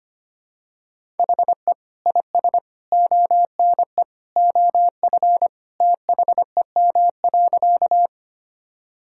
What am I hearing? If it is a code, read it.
Morse code, heard as a keyed tone, about 25 wpm, 700 Hz. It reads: HE IS ONE OF THEM.
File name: practice / 25wpm / Sentences with 3-Word Phrases - 25wpm